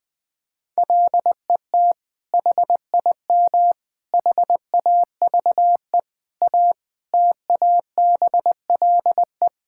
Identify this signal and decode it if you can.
Morse code, heard as a keyed tone, about 20 wpm, 700 Hz. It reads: LET HIM HAVE A TABLE
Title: Sentences from Top 1600 Words 20wpm